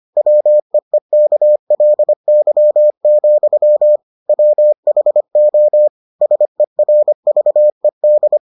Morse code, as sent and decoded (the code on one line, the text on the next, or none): .-- . -.- .-.. -.-- --..-- / .-- .... --- / ... .-. ...- . -..
WEEKLY, WHO SERVED